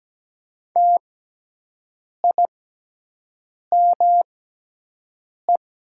Morse code, heard as a keyed tone, about 17 wpm, spaced as for 6 wpm, 700 Hz. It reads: TIME
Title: 6x Character Spacing - Top 100 Words - 17wpm